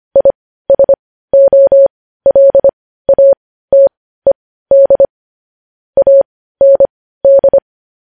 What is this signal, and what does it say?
Morse code, heard as a keyed tone, about 25 wpm, spaced as for 15 wpm, 550 Hz. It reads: ISOLATED AND
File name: podcast / News headlines-1x-25x15